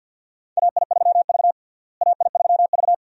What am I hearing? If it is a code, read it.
Morse code, heard as a keyed tone, about 53 wpm, 700 Hz. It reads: AI3V AI3V
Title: USA Call Signs - 4 Characters - Repeated 2x - 50wpm